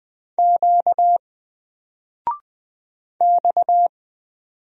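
Morse code, heard as a keyed tone, about 20 wpm, 700 Hz. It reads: Q X
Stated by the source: Single letter and number Morse drill at 20 wpm, from N4LCD.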